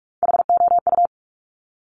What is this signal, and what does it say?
Morse code, heard as a keyed tone, about 46 wpm, 700 Hz. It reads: HOU
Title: Top Three Letter Combinations in Words 45wpm